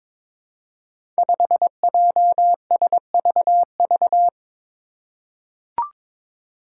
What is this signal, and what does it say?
Morse code, heard as a keyed tone, about 22 wpm, 700 Hz. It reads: 5JSVV
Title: Five Challenging Letter-Number Combinations 22wpm